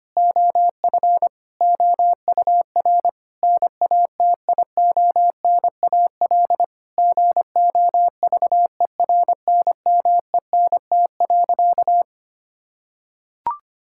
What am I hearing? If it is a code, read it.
Morse code, heard as a keyed tone, about 25 wpm, 700 Hz. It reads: OF OUR NATIONAL GOVERNMENT.
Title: Sentences from Top 1000 Words - Review Part 5 - 25wpm